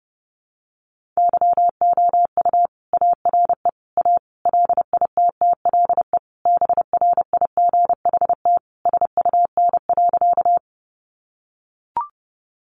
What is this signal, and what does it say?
Morse code, heard as a keyed tone, about 30 wpm, 700 Hz. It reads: YOU ARE A LITTLE BRIGHT SUN.
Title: Sentences from Top 1000 Words 30wpm